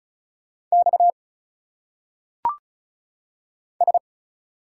Morse code, heard as a keyed tone, about 36 wpm, 700 Hz. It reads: X S
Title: Single Letters - Phonetic Pronunciation 35wpm